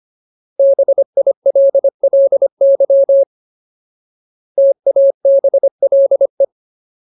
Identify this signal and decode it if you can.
Morse code, heard as a keyed tone, about 25 wpm, 550 Hz. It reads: BILLY TABLE
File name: Ultimate Word List / 25WPM Ultimate